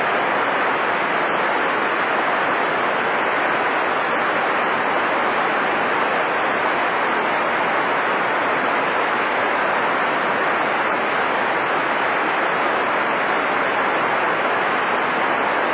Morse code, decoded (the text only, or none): none